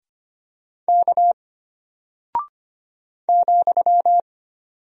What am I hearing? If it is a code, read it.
Morse code, heard as a keyed tone, about 25 wpm, 700 Hz. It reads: K ,
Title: Single Character (Letter:Number:Punctuation:Prosign) Alt 25wpm